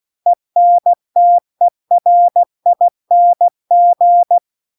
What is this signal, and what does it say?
Morse code, heard as a keyed tone, about 16 wpm, 700 Hz. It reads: ENTERING
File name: CW-The-Big-Twoheaded-River-2-16WPM